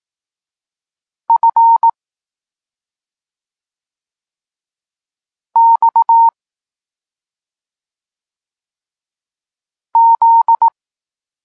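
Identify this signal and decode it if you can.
Morse code, heard as a keyed tone, about 18 wpm, 920 Hz. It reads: F X Z
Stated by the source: Morse code lesson for learning the letters K and X, from the SP9MOA Niepołomicki Klub Krótkofalowców course.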